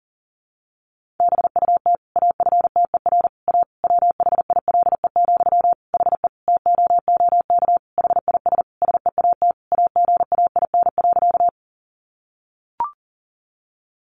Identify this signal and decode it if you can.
Morse code, heard as a keyed tone, about 40 wpm, 700 Hz. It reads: BUT AFTER A WHILE, HE TOOK HIS SEAT AGAIN.
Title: Sentences from Top 1500 Words 40wpm